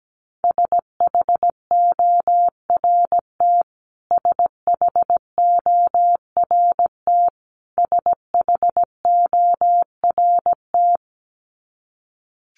Morse code, heard as a keyed tone, about 17 wpm, 700 Hz. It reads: SHORT SHORT SHORT
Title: Top 500 Words Repeated 17wpm